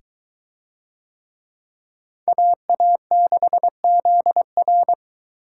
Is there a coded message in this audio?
Yes: Morse code from 2.3 to 4.9 s, heard as a keyed tone, about 23 wpm, 700 Hz. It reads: AA6ZR